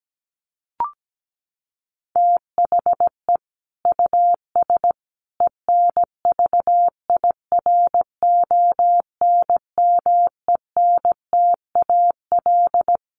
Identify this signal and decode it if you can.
Morse code, heard as a keyed tone, about 17 wpm, 700 Hz. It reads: THE US ENVIRONMENTAL
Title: Sets of 5 Words Encore2 17wpm